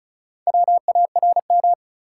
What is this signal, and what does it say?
Morse code, heard as a keyed tone, about 34 wpm, 700 Hz. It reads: WARM
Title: Four Letter Words - 35wpm